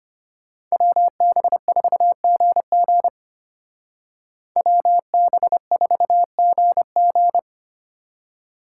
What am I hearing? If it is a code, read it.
Morse code, heard as a keyed tone, about 32 wpm, 700 Hz. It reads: WB4GG WB4GG